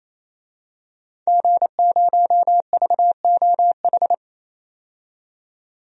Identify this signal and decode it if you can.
Morse code, heard as a keyed tone, about 28 wpm, 700 Hz. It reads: G0VOH
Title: DXCC-100 Call Signs - 5 Characters - 28wpm